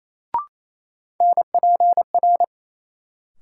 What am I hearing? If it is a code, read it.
Morse code, heard as a keyed tone, about 28 wpm, 700 Hz. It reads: NPR
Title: Sets of 5 Words - New Word Drill - 28wpm